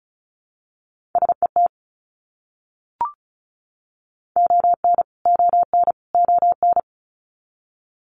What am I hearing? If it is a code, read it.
Morse code, heard as a keyed tone, about 34 wpm, 700 Hz. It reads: SET ON ON ON
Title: Top 300 Words Repeated 35wpm